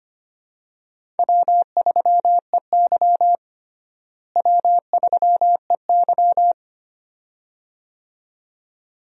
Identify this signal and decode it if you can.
Morse code, heard as a keyed tone, about 25 wpm, 700 Hz. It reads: W3EY W3EY
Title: USA Call Signs - 4 Characters - Repeated 2x - 25wpm